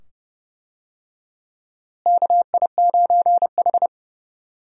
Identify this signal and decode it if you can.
Morse code, heard as a keyed tone, about 29 wpm, 700 Hz. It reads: KI9H
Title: USA Call Signs - 4 Characters - Repeated 2x - 30wpm